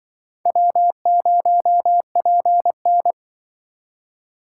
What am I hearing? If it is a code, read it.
Morse code, heard as a keyed tone, about 25 wpm, 700 Hz. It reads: W0PN